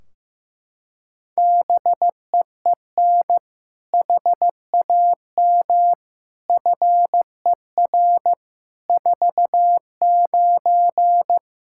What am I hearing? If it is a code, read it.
Morse code, heard as a keyed tone, about 15 wpm, 700 Hz. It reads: BEEN HAM FER 49
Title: Practice QSO - Exchange 3 - More Info - 15wpm